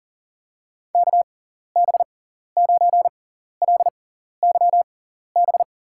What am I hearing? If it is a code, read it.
Morse code, heard as a keyed tone, about 40 wpm, spaced as for 14 wpm, 700 Hz. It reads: KB9LYB
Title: USA Call Signs - 6x Character Spacing - 40wpm